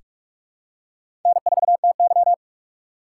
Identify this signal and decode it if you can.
Morse code, heard as a keyed tone, about 43 wpm, 700 Hz. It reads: N4TY
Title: USA Call Signs - 4x Character Spacing - 45wpm